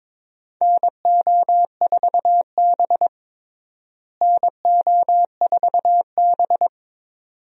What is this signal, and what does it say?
Morse code, heard as a keyed tone, about 22 wpm, 700 Hz. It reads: NO4B NO4B
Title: USA Call Signs - 4 Characters - Repeated 2x - 22wpm